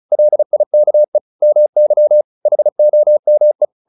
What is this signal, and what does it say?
Morse code, heard as a keyed tone, about 34 wpm, 600 Hz. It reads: LIKE MY HOME